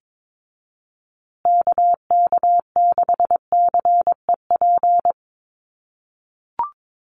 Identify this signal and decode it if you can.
Morse code, heard as a keyed tone, about 22 wpm, 700 Hz. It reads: KK6CEP
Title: USA Call Signs Repeated 3x 22wpm